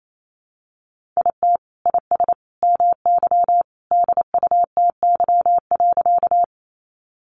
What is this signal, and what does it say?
Morse code, heard as a keyed tone, about 28 wpm, 700 Hz. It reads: IT IS MY DUTY.
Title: Sentences from Top 1500 Words 28wpm